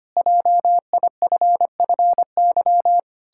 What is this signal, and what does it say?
Morse code, heard as a keyed tone, about 25 wpm, 700 Hz. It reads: JIFFY
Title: Pangram Challenge v2 - 25wpm